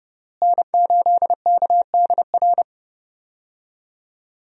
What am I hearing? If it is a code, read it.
Morse code, heard as a keyed tone, about 30 wpm, 700 Hz. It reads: N8KDR
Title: USA Call Signs - 5 Characters - Repeated 3x - 30wpm